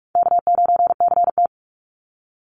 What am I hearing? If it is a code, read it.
Morse code, heard as a keyed tone, about 46 wpm, 700 Hz. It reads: K9CT
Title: Easy Call Signs - 45wpm